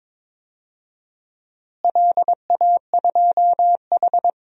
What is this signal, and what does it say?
Morse code, heard as a keyed tone, about 22 wpm, 700 Hz. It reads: LA2H
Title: DXCC-100 Call Signs - 4 Characters - Repeated 2x - 22wpm